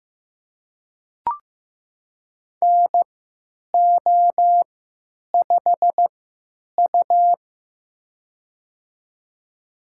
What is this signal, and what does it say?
Morse code, heard as a keyed tone, about 15 wpm, spaced as for 9 wpm, 700 Hz. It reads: NO5U
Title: USA Call Signs - 4 Characters - 3x Character Spacing - 15wpm